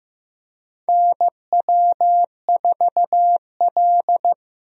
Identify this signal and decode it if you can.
Morse code, heard as a keyed tone, about 15 wpm, 700 Hz. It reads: NW4L
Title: USA Call Signs - 4 Characters - Repeated 3x - 15wpm